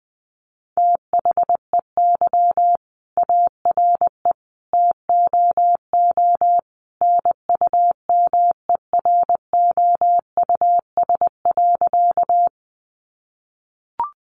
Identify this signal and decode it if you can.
Morse code, heard as a keyed tone, about 20 wpm, 700 Hz. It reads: THEY ARE TOO NUMEROUS.
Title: Sentences from Top 1400 Words 20wpm